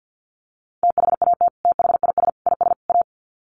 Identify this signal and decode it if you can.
Morse code, heard as a keyed tone, about 50 wpm, 700 Hz. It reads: THAT THIS IS A
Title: Sets of 4 Words 50wpm